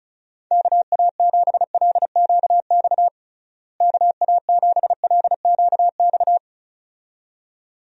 Morse code, heard as a keyed tone, about 36 wpm, 700 Hz. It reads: KA7LQX KA7LQX
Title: USA Call Signs - 6 Characters - Repeated 2x - 35wpm